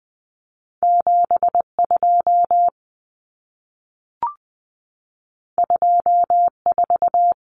Morse code, heard as a keyed tone, about 20 wpm, 700 Hz. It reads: 72 24
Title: Common Cut Numbers - 2-Digit 20wpm